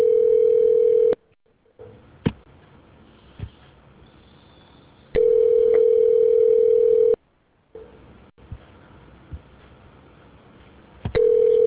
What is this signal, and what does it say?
Ringback tone (North America), 440 + 480 Hz.